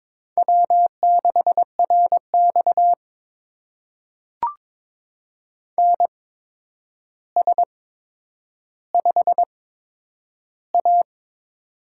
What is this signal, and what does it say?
Morse code, heard as a keyed tone, about 22 wpm, spaced as for 9 wpm, 700 Hz. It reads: W6RX NS5A
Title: USA Call Signs - 4 Characters - 8x Character Spacing - 22wpm